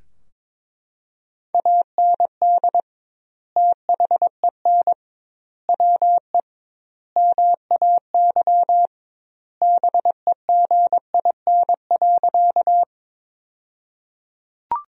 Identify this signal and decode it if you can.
Morse code, heard as a keyed tone, about 22 wpm, 700 Hz. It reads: AND THEN WE MAY BEGIN.
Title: 2x Word-Spacing - Sentences from Top 300 Words - 22wpm